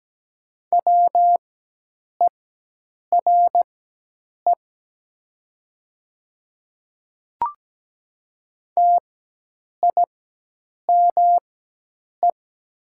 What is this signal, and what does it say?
Morse code, heard as a keyed tone, about 17 wpm, spaced as for 8 wpm, 690 Hz. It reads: WERE TIME
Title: 4x Character Spacing - Top 100 Words - 17wpm